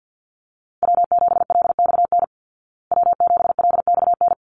Morse code, heard as a keyed tone, about 49 wpm, 700 Hz. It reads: W7LXN W7LXN
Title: Call Signs Repeated 3x - 50wpm